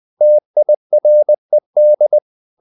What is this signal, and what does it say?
Morse code, heard as a keyed tone, about 20 wpm, 600 Hz. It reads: TIRED
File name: the old man and the sea part 2 20wpm